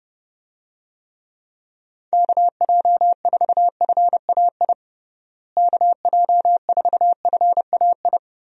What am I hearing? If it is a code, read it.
Morse code, heard as a keyed tone, about 30 wpm, 700 Hz. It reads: KJ4FAI KJ4FAI
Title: USA Call Signs - 6 Characters - Repeated 3x - 30wpm